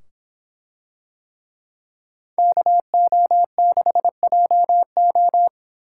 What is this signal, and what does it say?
Morse code, heard as a keyed tone, about 24 wpm, 700 Hz. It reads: KO6JO